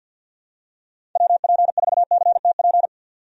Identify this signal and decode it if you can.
Morse code, heard as a keyed tone, about 50 wpm, 700 Hz. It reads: WP4CTP
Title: USA Call Signs - 6 Characters - 50wpm